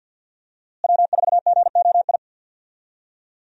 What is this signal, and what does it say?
Morse code, heard as a keyed tone, about 52 wpm, 700 Hz. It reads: W4GOI
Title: USA Call Signs - 5 Characters - Repeated 3x - 50wpm